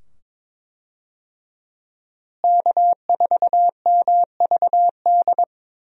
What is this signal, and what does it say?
Morse code, heard as a keyed tone, about 22 wpm, 700 Hz. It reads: K4MVD
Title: USA Call Signs - 5 Characters - Repeated 2x - 22wpm